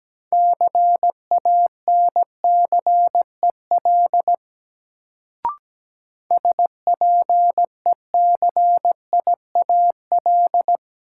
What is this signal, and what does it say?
Morse code, heard as a keyed tone, about 17 wpm, 700 Hz. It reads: CANCEL SPECIAL
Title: Sentences from Top 1200 Words - New Word Drill Alt - 17wpm